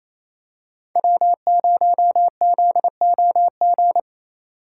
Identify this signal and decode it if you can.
Morse code, heard as a keyed tone, about 28 wpm, 700 Hz. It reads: W0ZOG
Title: USA Call Signs - 5 Characters - 8x Character Spacing - 28wpm